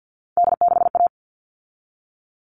Morse code, heard as a keyed tone, about 50 wpm, 700 Hz. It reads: D6A
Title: All Cut Numbers - 3-Digit 50wpm